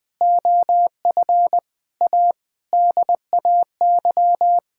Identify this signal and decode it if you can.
Morse code, heard as a keyed tone, about 20 wpm, 700 Hz. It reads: OF A DAY
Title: Sentences with 3-Word Phrases - 20wpm